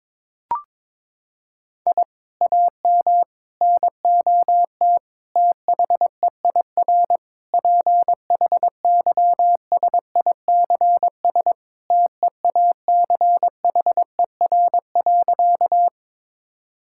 Morse code, heard as a keyed tone, about 22 wpm, 700 Hz. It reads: I AM NOT THEIR PHYSICS TEACHER.
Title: Sentences from Top 1000 Words - Review Part 6 - 22wpm